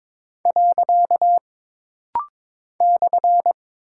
Morse code, heard as a keyed tone, about 22 wpm, 700 Hz. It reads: . /
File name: practice / 22wpm / Single Punctuation-Prosign - Rapid-Fire - 22wpm